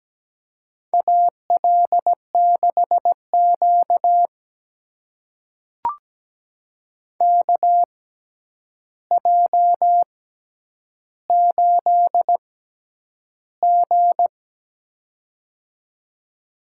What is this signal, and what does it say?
Morse code, heard as a keyed tone, about 17 wpm, spaced as for 9 wpm, 700 Hz. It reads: AL6Q KJ8G
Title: USA Call Signs - 4 Characters - 6x Character Spacing - 17wpm